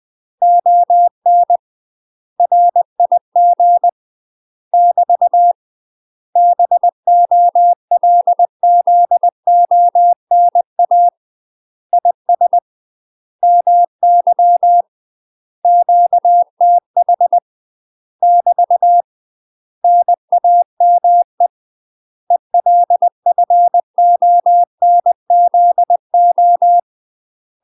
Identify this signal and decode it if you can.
Morse code, heard as a keyed tone, about 20 wpm, 700 Hz. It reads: ON RIG <BT> BOLZONA IS MY QTH <BT> NAME ELFONZO
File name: UK MASTER QSOs TRAINING / 20WPM / UK MASTER QSOs 0060